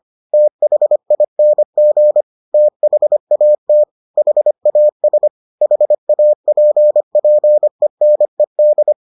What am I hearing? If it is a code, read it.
Morse code, heard as a keyed tone, about 25 wpm, 600 Hz. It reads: THING THAT HAS HAPPENED